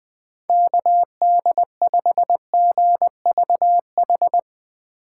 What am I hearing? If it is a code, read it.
Morse code, heard as a keyed tone, about 21 wpm, 700 Hz. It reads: KD5GVH
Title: Speed Racing - USA 2x3 Call Signs 17wpm